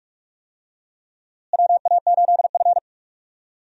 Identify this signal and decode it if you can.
Morse code, heard as a keyed tone, about 43 wpm, 700 Hz. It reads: WA8F
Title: USA Call Signs - 4 Characters - Repeated 3x - 45wpm